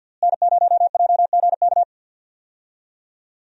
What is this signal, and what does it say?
Morse code, heard as a keyed tone, about 48 wpm, 700 Hz. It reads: N0JGK
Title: USA Call Signs - 5 Characters - Repeated 2x - 50wpm